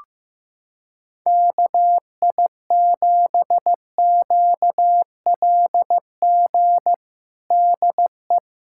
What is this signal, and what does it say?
Morse code, heard as a keyed tone, about 16 wpm, 700 Hz. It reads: KI7QLG DE
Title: USA Callsign DE Callsign - 6 Characters - 15wpm